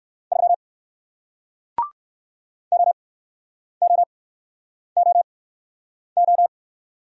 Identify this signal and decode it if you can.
Morse code, heard as a keyed tone, about 59 wpm, spaced as for 9 wpm, 700 Hz. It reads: 3 OOOO